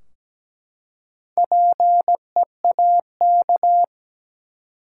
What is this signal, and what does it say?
Morse code, heard as a keyed tone, about 17 wpm, 700 Hz. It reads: PEAK
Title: Four Letter Words - 17wpm